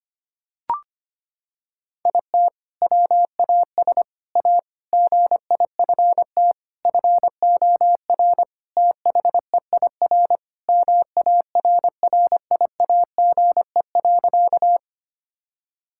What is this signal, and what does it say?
Morse code, heard as a keyed tone, about 25 wpm, 700 Hz. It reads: IT WAS A GIFT FOR THEIR MARRIAGE.
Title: Sentences from Top 1000 Words - Review Part 6 - 25wpm